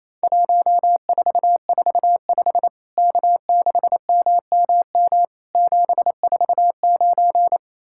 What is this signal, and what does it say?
Morse code, heard as a keyed tone, about 28 wpm, 700 Hz. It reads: 1445 K6MMM 749
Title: Worked All Europe DX Contest - QTC Exchange - 28wpm